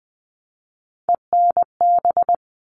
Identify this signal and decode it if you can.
Morse code, heard as a keyed tone, about 20 wpm, 700 Hz. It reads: ENB
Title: All Cut Numbers - 3-Digit 20wpm